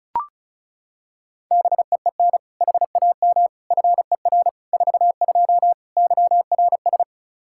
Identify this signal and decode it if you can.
Morse code, heard as a keyed tone, about 34 wpm, 700 Hz. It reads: BEEN HAM FER 42 YRS
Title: Practice QSO - Exchange 3 - More Info - 35wpm